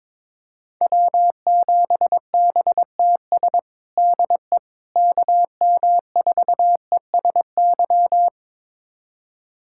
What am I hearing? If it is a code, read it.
Morse code, heard as a keyed tone, about 21 wpm, 700 Hz. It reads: W7BTS DE KM4ESY
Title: USA Callsign DE Callsign 22wpm